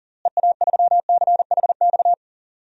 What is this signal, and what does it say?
Morse code, heard as a keyed tone, about 40 wpm, 700 Hz. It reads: EA3CHX
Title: DXCC-100 Call Signs - 6 Characters - Repeated 2x - 40wpm